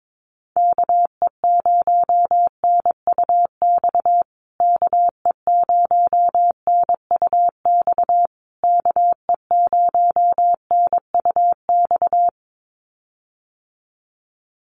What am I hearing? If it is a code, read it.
Morse code, heard as a keyed tone, about 22 wpm, 700 Hz. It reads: KE0NUX KE0NUX KE0NUX
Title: USA Call Signs Repeated 3x 22wpm